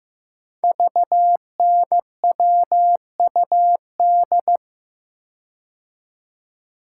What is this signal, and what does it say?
Morse code, heard as a keyed tone, about 15 wpm, 700 Hz. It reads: VNWUD
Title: Code Groups - 5 Letters - 15wpm